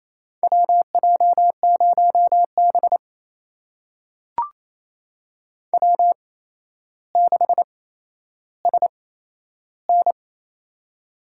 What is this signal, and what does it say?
Morse code, heard as a keyed tone, about 28 wpm, spaced as for 12 wpm, 700 Hz. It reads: WJ0B W6SN
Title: USA Call Signs - 4 Characters - 8x Character Spacing - 28wpm